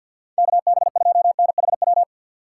Morse code, heard as a keyed tone, about 52 wpm, 700 Hz. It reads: KB2NHW
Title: USA Call Signs - 6 Characters - 4x Character Spacing - 50wpm